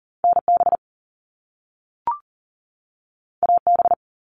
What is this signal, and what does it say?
Morse code, heard as a keyed tone, about 40 wpm, 700 Hz. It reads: NB AB
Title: Canadian Provinces and Territories 40wpm